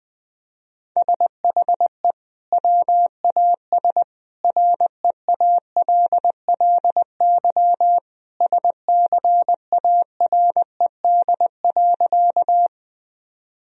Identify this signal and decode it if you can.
Morse code, heard as a keyed tone, about 20 wpm, 700 Hz. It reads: SHE WAS REALLY SCARED.
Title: Sentences from Top 1000 Words - Review Part 4 - 20wpm